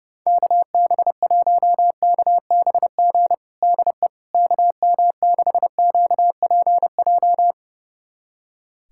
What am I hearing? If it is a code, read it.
Morse code, heard as a keyed tone, about 32 wpm, 700 Hz. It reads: KB1KBG DE KM6QPJ
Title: USA Callsign DE Callsign 30wpm